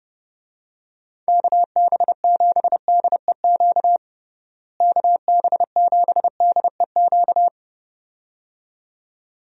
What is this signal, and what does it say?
Morse code, heard as a keyed tone, about 30 wpm, 700 Hz. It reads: KB7DEQ KB7DEQ
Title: USA Call Signs - 6 Characters - Repeated 2x - 30wpm